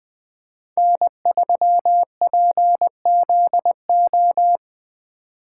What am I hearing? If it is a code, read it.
Morse code, heard as a keyed tone, about 22 wpm, 700 Hz. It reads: N3PZO
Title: Speed Racing - USA Call Signs 20wpm